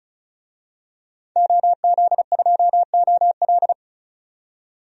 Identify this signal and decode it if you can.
Morse code, heard as a keyed tone, about 34 wpm, 700 Hz. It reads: OZ2OL
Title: DXCC-100 Call Signs - 5 Characters - 35wpm